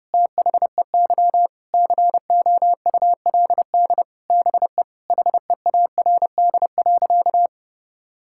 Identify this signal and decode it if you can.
Morse code, heard as a keyed tone, about 30 wpm, 700 Hz. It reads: THEY COULD BE HEARD.
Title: Sentences from Top 1000 Words - Review Part 5 - 30wpm